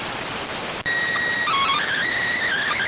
SSTV (slow-scan television), PD 290.